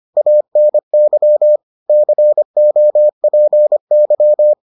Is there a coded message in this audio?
Yes: Morse code from 0.2 to 4.5 s, heard as a keyed tone, about 25 wpm, 600 Hz. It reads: ANY COPY